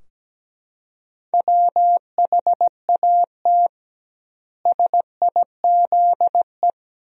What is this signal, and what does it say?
Morse code, heard as a keyed tone, about 17 wpm, 700 Hz. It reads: WHAT SIZE
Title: 2x Word-Spacing - Sentences from Top 400 Words - 17wpm